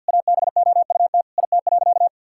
Morse code, heard as a keyed tone, about 46 wpm, 700 Hz. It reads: ABOUT IT.
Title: Sentences from Top 1600 Words 50wpm